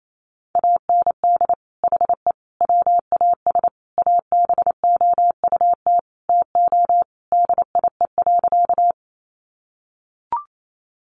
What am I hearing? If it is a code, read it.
Morse code, heard as a keyed tone, about 28 wpm, 700 Hz. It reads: AND HE WAS ABOUT TO DIE.
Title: Sentences from Top 1300 Words 28wpm